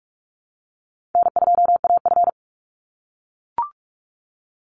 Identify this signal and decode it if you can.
Morse code, heard as a keyed tone, about 44 wpm, 700 Hz. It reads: N2AF
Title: NA SOTA Chaser Callsigns 45wpm